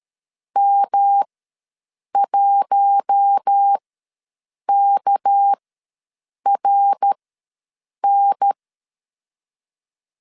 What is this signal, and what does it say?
Morse code, heard as a keyed tone, about 13 wpm, spaced as for 7 wpm, 800 Hz. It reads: M1KRN
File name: LEZIONI / 6 PKX0 / 6 PKX0 ws 12wpm 3448